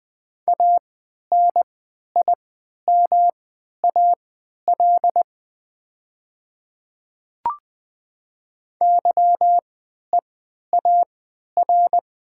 Morse code, heard as a keyed tone, about 20 wpm, spaced as for 11 wpm, 700 Hz. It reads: ANIMAL YEAR
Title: 3x Character Spacing - Top 300 Words - 20wpm